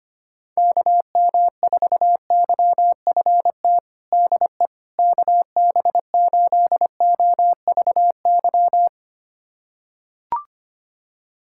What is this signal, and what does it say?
Morse code, heard as a keyed tone, about 25 wpm, 700 Hz. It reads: KM4YFT DE KB8OVY
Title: USA Callsign DE Callsign - 6 Characters - 25wpm